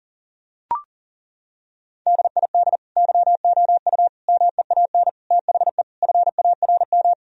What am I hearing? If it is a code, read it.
Morse code, heard as a keyed tone, about 40 wpm, 700 Hz. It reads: DID YOU MEAN THE FARM
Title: Sentences from Top 1000 Words - Review Part 2 - 40wpm